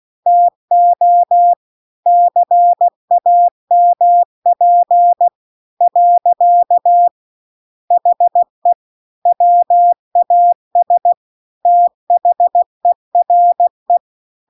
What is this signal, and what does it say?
Morse code, heard as a keyed tone, about 16 wpm, 700 Hz. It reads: TO CAMP . HE WAS THERE